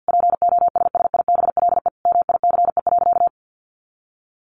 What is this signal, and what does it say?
Morse code, heard as a keyed tone, about 50 wpm, 700 Hz. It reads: POSSIBLE MICE.
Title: A Sound Of Thunder 50wpm